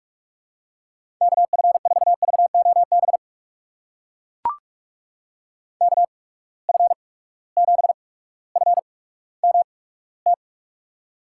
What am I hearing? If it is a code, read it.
Morse code, heard as a keyed tone, about 44 wpm, spaced as for 19 wpm, 700 Hz. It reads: KF4VOB KF7FMT
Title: USA Call Signs - 6 Characters - 8x Character Spacing - 45wpm